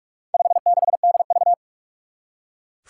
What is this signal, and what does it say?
Morse code, heard as a keyed tone, about 42 wpm, 700 Hz. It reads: 56DV
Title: Four Challenging Letter-Number Combinations 45wpm